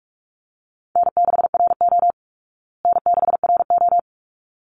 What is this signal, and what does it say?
Morse code, heard as a keyed tone, about 44 wpm, 700 Hz. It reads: N6RO N6RO
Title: Easy Call Signs Repeated 2x - 45wpm